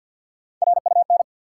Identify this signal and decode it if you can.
Morse code, heard as a keyed tone, about 52 wpm, 700 Hz. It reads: RUN
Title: Sets of 4 Words - New Word Drill - 50wpm